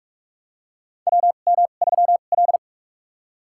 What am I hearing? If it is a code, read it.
Morse code, heard as a keyed tone, about 42 wpm, spaced as for 31 wpm, 700 Hz. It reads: WM3L
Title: USA Call Signs - 4 Characters - 2x Character Spacing - 45wpm